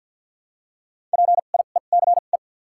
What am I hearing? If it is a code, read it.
Morse code, heard as a keyed tone, about 48 wpm, spaced as for 35 wpm, 700 Hz. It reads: PIECE
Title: 2x Character Spacing - Top 500 Words - 50wpm